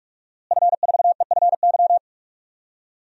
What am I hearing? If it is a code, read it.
Morse code, heard as a keyed tone, about 46 wpm, 700 Hz. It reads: F4EFY
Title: DXCC-100 Call Signs - 5 Characters - Repeated 3x - 45wpm